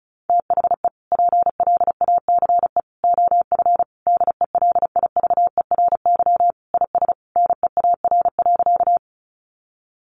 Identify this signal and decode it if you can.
Morse code, heard as a keyed tone, about 34 wpm, 700 Hz. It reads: THE PLACE OF DELIVERY IS NEAR.
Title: Sentences from Top 1500 Words 35wpm